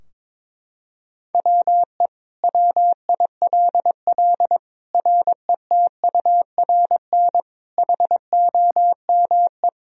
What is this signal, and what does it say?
Morse code, heard as a keyed tone, about 22 wpm, 700 Hz. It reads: WE WILL RETURN HOME
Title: Sentences from Top 1000 Words - Review Part 3 - 22wpm